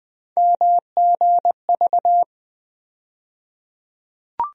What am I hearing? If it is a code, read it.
Morse code, heard as a keyed tone, about 20 wpm, 700 Hz. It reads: MGV
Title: USA Call Sign Postfixes - 3 Letters 20wpm